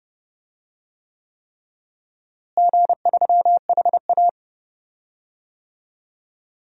Morse code, heard as a keyed tone, about 30 wpm, 700 Hz. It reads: G3HA
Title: DXCC-100 Call Signs - 4 Characters - Repeated 3x - 30wpm